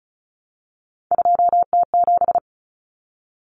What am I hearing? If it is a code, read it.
Morse code, heard as a keyed tone, about 34 wpm, 700 Hz. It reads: 2T7
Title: Common Cut Numbers - 3-Digit 35wpm